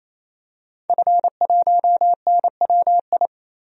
Morse code, heard as a keyed tone, about 27 wpm, 700 Hz. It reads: F1NWI
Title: DXCC-100 Call Signs - 5 Characters - 28wpm